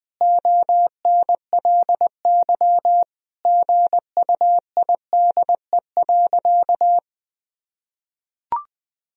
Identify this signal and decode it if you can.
Morse code, heard as a keyed tone, about 20 wpm, 700 Hz. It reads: ONLY GUIDE.
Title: Sentences from Top 1000 Words - Review Part 3 - 20wpm